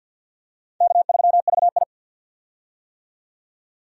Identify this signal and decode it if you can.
Morse code, heard as a keyed tone, about 47 wpm, 700 Hz. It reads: K3VI
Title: USA Call Signs - 4 Characters - 50wpm